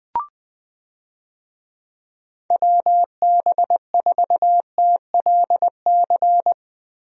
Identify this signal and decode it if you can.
Morse code, heard as a keyed tone, about 20 wpm, 700 Hz. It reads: WB4TLC
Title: USA Call Signs - 6 Characters - Repeated 2x - 20wpm